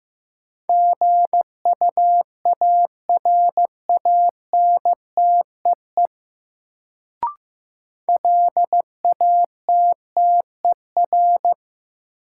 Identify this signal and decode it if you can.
Morse code, heard as a keyed tone, about 15 wpm, 700 Hz. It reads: GUARANTEE LATTER
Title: Sentences from Top 1600 Words - New Word Drill Alt - 15wpm